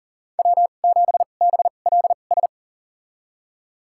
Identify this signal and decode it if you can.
Morse code, heard as a keyed tone, about 40 wpm, spaced as for 29 wpm, 700 Hz. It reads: W7BLS
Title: USA Call Signs - 5 Characters - 2x Character Spacing - 40wpm